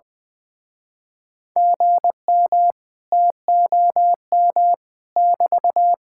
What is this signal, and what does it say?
Morse code, heard as a keyed tone, about 20 wpm, 700 Hz. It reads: GM TOM <BT>
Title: Speed-Racing Common QSO Phrases - 20wpm